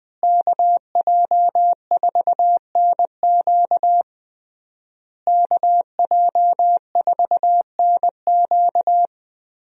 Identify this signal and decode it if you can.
Morse code, heard as a keyed tone, about 19 wpm, 700 Hz. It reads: KJ4NQ KJ4NQ
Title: USA Call Signs - 5 Characters - Repeated 3x - 20wpm